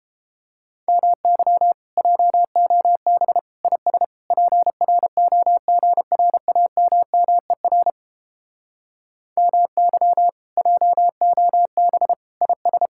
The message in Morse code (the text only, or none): MY JOB IS PROGRAMMER MY JOB IS